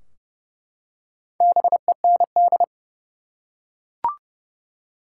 Morse code, heard as a keyed tone, about 30 wpm, 700 Hz. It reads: BEND
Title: Four Letter Words - 30wpm